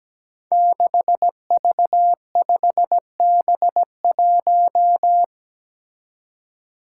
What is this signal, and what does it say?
Morse code, heard as a keyed tone, about 17 wpm, 700 Hz. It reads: 6V5B1
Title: Five Challenging Letter-Number Combinations 17wpm